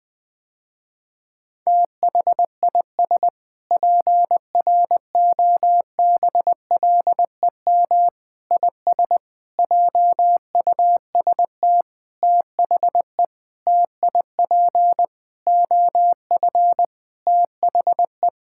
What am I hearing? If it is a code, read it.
Morse code, heard as a keyed tone, about 20 wpm, 700 Hz. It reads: THIS PROBLEM IS JUST THE TIP OF THE
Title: Sentences with 7-Word Phrases - Spoken Beginning - 20wpm